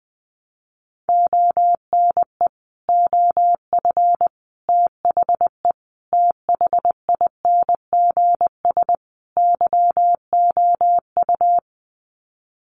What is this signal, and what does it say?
Morse code, heard as a keyed tone, about 20 wpm, 700 Hz. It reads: ONE OF THE THINGS YOU
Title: Sets of 5 Words Encore2 20wpm